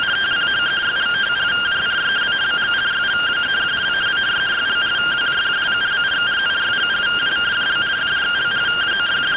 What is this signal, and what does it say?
Two-tone teleprinter signal, about 170 Hz shift.